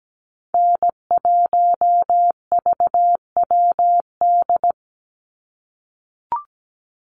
Morse code, heard as a keyed tone, about 17 wpm, 700 Hz. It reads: N1VWD
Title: NA SOTA Chaser Callsigns 17wpm